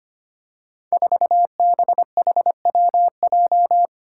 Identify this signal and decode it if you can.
Morse code, heard as a keyed tone, about 25 wpm, 700 Hz. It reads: 4BHWJ
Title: Five Challenging Letter-Number Combinations 25wpm